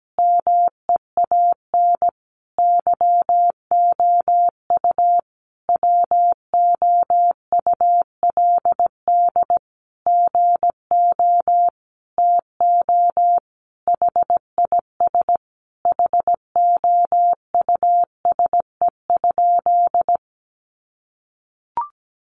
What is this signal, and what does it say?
Morse code, heard as a keyed tone, about 17 wpm, 700 Hz. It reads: MEAN YOU WOULD GO TO HIS HOUSE?
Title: Sentences from Top 600 Words 17wpm